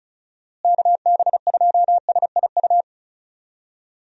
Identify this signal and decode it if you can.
Morse code, heard as a keyed tone, about 34 wpm, 700 Hz. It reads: KB2SIU
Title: USA Call Signs - 6 Characters - Repeated 2x - 35wpm